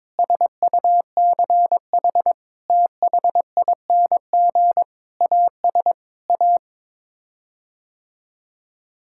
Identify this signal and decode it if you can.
Morse code, heard as a keyed tone, about 22 wpm, 700 Hz. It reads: SUCH THING AS A
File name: practice / 22wpm / Sets of 8 Words - 22wpm